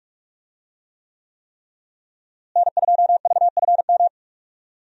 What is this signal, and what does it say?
Morse code, heard as a keyed tone, about 41 wpm, 700 Hz. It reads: N2VFM
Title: USA Call Signs - 5 Characters - Repeated 3x - 45wpm